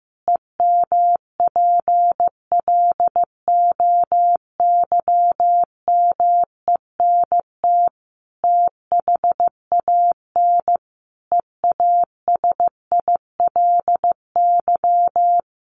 Morse code, heard as a keyed tone, about 15 wpm, 700 Hz. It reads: EMPLOYMENT THAN EASILY
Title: Sentences from Top 1500 Words 15wpm